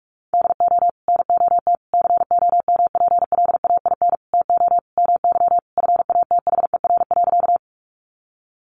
Morse code, heard as a keyed tone, about 44 wpm, 700 Hz. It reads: DO NOT COMPLAIN TO MY FATHER.
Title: Sentences from Top 1300 Words 45wpm